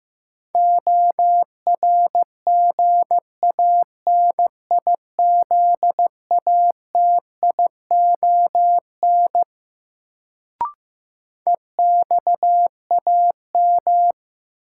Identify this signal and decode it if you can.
Morse code, heard as a keyed tone, about 15 wpm, 700 Hz. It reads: ORGANIZATION EXAM
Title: Sentences from Top 1000 Words - Review Part 5 - New Word Drill - 15wpm